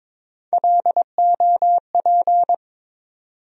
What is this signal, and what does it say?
Morse code, heard as a keyed tone, about 22 wpm, 700 Hz. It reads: LOP
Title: USA Call Sign Postfixes - 3 Letters 22wpm